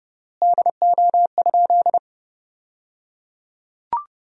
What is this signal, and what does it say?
Morse code, heard as a keyed tone, about 30 wpm, 700 Hz. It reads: DO?
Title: 3x Word-Spacing - Sentences from Top 100 Words - 30wpm